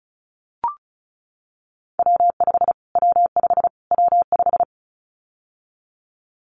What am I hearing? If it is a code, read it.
Morse code, heard as a keyed tone, about 34 wpm, 700 Hz. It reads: W5 W5 W5
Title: Callsign Prefixes for Top 25 DXCC - Repeated 3x - 35wpm